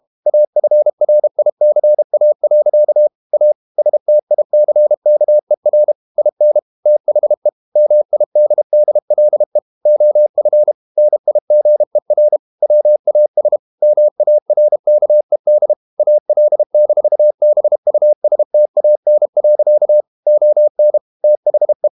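Morse code, heard as a keyed tone, about 32 wpm, 600 Hz. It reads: AFRICA. A STICKER IN THE MIDDLE OF NIGER WAS MARKED AL-BUSTAN. ON THE
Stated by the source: Morse code practice at 32 wpm from CW med Gnister, reading Ken Follett's novel Never in English.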